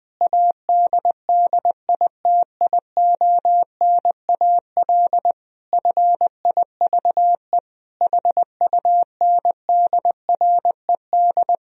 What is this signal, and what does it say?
Morse code, heard as a keyed tone, about 21 wpm, 700 Hz. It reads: ADDITIONAL FIVE HUNDRED
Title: Sentences from Top 1000 Words - Review Part 4 - 20wpm